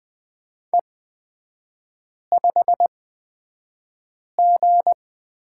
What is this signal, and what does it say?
Morse code, heard as a keyed tone, about 20 wpm, spaced as for 5 wpm, 700 Hz. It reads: E5G
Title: 250 Lesson 40e - BK - Three Characters 20wpm